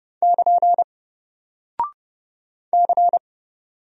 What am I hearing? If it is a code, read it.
Morse code, heard as a keyed tone, about 30 wpm, 700 Hz. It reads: <KN> C
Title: Single Character (Letter:Number:Punctuation:Prosign) 30wpm